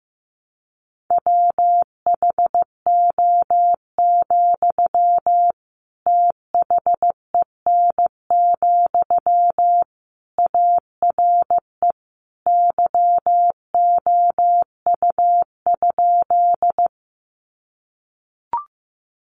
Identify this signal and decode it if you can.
Morse code, heard as a keyed tone, about 15 wpm, 700 Hz. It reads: WHO, THEN, ARE YOU?
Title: Sentences from Top 100 Words 15wpm